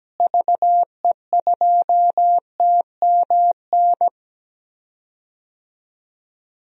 Morse code, heard as a keyed tone, about 17 wpm, 700 Hz. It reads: VE2TMN
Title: DXCC-100 Call Signs - 6 Characters - Repeated 2x - 17wpm